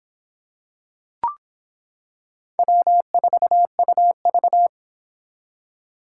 Morse code, heard as a keyed tone, about 25 wpm, 700 Hz. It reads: W4UV